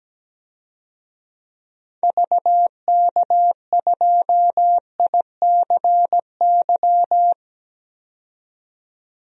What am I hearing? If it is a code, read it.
Morse code, heard as a keyed tone, about 17 wpm, 700 Hz. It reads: VK2ICY
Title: DXCC-100 Call Signs - 6 Characters - 17wpm